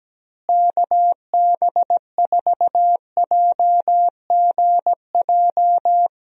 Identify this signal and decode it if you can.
Morse code, heard as a keyed tone, about 17 wpm, 700 Hz. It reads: KB4JGJ
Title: USA Call Signs - 6 Characters - Repeated 3x - 17wpm